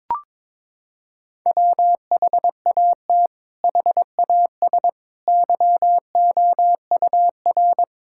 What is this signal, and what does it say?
Morse code, heard as a keyed tone, about 22 wpm, 700 Hz. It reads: WHAT HAS YOUR
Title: Sentences from Top 1600 Words 22wpm